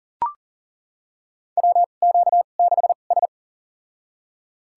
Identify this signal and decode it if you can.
Morse code, heard as a keyed tone, about 40 wpm, spaced as for 29 wpm, 700 Hz. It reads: WQ6S
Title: USA Call Signs - 4 Characters - 2x Character Spacing - 40wpm